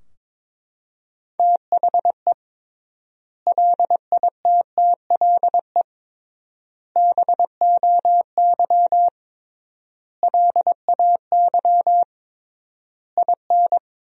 Morse code, heard as a keyed tone, about 22 wpm, 700 Hz. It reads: THE LITTLE BOY LAY IN
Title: 3x Word-Spacing - Sentences from Top 500 Words - 22wpm